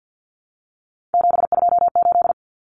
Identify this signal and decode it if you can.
Morse code, heard as a keyed tone, about 50 wpm, 700 Hz. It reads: 728
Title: Common Cut Numbers - 3-Digit 50wpm